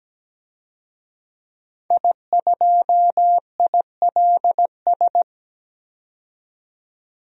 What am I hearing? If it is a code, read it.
Morse code, heard as a keyed tone, about 17 wpm, 700 Hz. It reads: I2ILS
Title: DXCC-100 Call Signs - 5 Characters - 17wpm